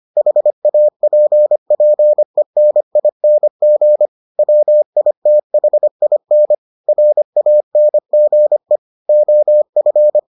Morse code, heard as a keyed tone, about 25 wpm, 600 Hz. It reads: HAPPENING WITHIN RANGE OF